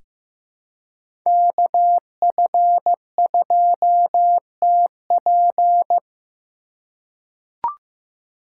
Morse code, heard as a keyed tone, about 15 wpm, 700 Hz. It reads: KF2TP
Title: K1USN SST QSO Practice - 15wpm